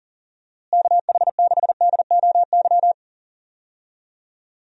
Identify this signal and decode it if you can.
Morse code, heard as a keyed tone, about 40 wpm, 700 Hz. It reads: KH6DOY
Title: DXCC-100 Call Signs - 6 Characters - Repeated 2x - 40wpm